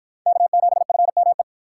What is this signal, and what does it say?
Morse code, heard as a keyed tone, about 49 wpm, 700 Hz. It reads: K7FME